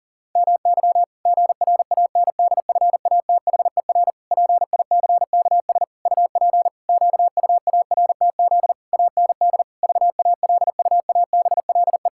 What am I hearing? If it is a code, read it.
Morse code, heard as a keyed tone, about 40 wpm, 700 Hz. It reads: MY GRANDFATHER PICKS UP QUARTZ AND VALUABLE
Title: Pangram Challenge v2 - 40wpm